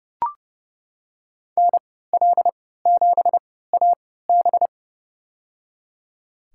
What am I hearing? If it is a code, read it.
Morse code, heard as a keyed tone, about 30 wpm, spaced as for 17 wpm, 700 Hz. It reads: NL7AB
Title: USA Call Signs - 5 Characters - 3x Character Spacing - 30wpm